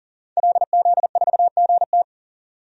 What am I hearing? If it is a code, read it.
Morse code, heard as a keyed tone, about 40 wpm, 700 Hz. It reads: LZ4GT
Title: DXCC-100 Call Signs - 5 Characters - 40wpm